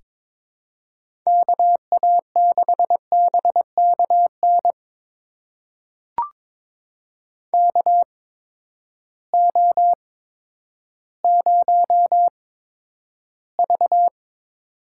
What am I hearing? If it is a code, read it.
Morse code, heard as a keyed tone, about 22 wpm, spaced as for 11 wpm, 700 Hz. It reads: KA6BKN KO0V
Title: USA Call Signs - 8x Character Spacing - 22wpm